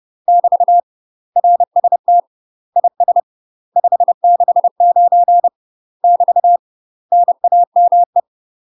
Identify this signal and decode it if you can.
Morse code, heard as a keyed tone, about 30 wpm, 700 Hz. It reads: <BT> RST IS 569 <BT> NAME
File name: UK MASTER QSOs TRAINING/30WPM